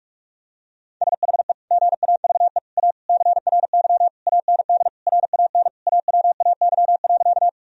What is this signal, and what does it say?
Morse code, heard as a keyed tone, about 46 wpm, 700 Hz. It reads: SHE GAVE A CRY AND RAN AWAY.
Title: Sentences from Top 1000 Words - Review Part 3 - 45wpm